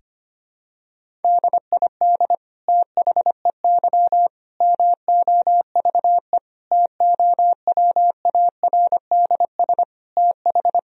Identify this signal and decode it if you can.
Morse code, heard as a keyed tone, about 25 wpm, 700 Hz. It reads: DID THEY MOVE TOWARDS TH
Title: Sentences from Top 1000 Words - Review Part 2 - 25wpm